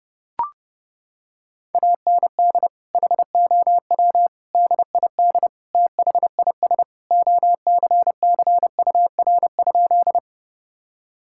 Morse code, heard as a keyed tone, about 28 wpm, 700 Hz. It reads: AND HOW DID THIS OCCUR?
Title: Sentences from Top 1000 Words - Review Part 6 - 30wpm